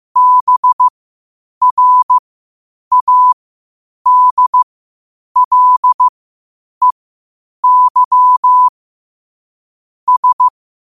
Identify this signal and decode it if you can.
Morse code, heard as a keyed tone, about 15 wpm, spaced as for 8 wpm, 1000 Hz. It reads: BRADLEY S
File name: qotdfiles / 8wpm